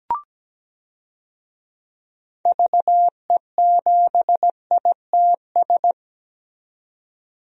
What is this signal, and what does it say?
Morse code, heard as a keyed tone, about 17 wpm, 700 Hz. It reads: VE7ITS
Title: DXCC-100 Call Signs - 6 Characters - 17wpm